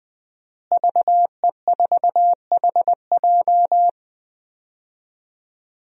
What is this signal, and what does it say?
Morse code, heard as a keyed tone, about 20 wpm, 700 Hz. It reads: VE4HJ
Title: DXCC-100 Call Signs - 5 Characters - Repeated 2x - 20wpm